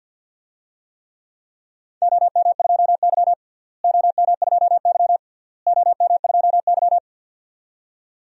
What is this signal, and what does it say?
Morse code, heard as a keyed tone, about 51 wpm, 700 Hz. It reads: OM2Y OM2Y OM2Y
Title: DXCC-100 Call Signs - 4 Characters - Repeated 3x - 50wpm